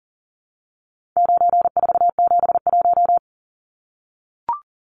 Morse code, heard as a keyed tone, about 40 wpm, 700 Hz. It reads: 9471
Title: Numbers - 4-Digit 40wpm